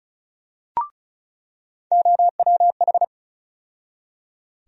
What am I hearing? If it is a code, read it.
Morse code, heard as a keyed tone, about 36 wpm, 700 Hz. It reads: OWH
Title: USA Call Sign Postfixes - 3 Letters 35wpm